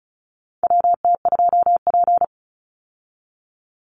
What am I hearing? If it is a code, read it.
Morse code, heard as a keyed tone, about 34 wpm, 700 Hz. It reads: WT2P
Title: Call Signs Repeated 2x - 35wpm